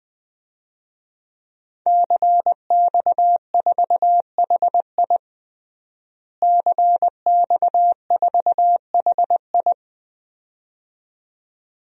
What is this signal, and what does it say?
Morse code, heard as a keyed tone, about 19 wpm, 700 Hz. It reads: CX4HI CX4HI
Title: DXCC-100 Call Signs - 5 Characters - Repeated 2x - 20wpm